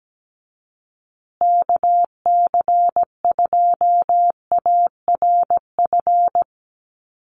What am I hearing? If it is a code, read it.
Morse code, heard as a keyed tone, about 17 wpm, 700 Hz. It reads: KC2ARF
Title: USA Call Signs Repeated 2x 17wpm